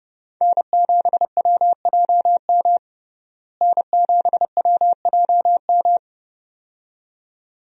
Morse code, heard as a keyed tone, about 31 wpm, 700 Hz. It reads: N7WJM N7WJM
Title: USA Call Signs - 5 Characters - Repeated 2x - 30wpm